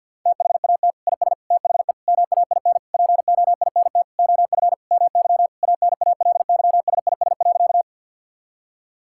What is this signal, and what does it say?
Morse code, heard as a keyed tone, about 51 wpm, 700 Hz. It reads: THAT IS THE MAIN POINT OF MY ANALYSIS.
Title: Sentences from Top 1000 Words - Review Part 5 - 50wpm